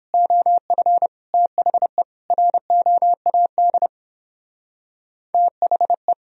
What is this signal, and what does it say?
Morse code, heard as a keyed tone, about 29 wpm, 700 Hz. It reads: OF THE ROAD THE